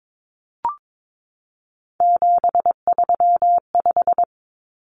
Morse code, heard as a keyed tone, about 22 wpm, 700 Hz. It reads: 735
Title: Common Cut Numbers - 3-Digit 22wpm